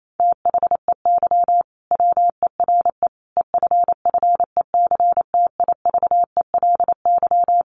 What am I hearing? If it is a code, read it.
Morse code, heard as a keyed tone, about 28 wpm, 700 Hz. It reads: THEY WERE EFFECTIVELY
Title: Sentences from Top 1400 Words 28wpm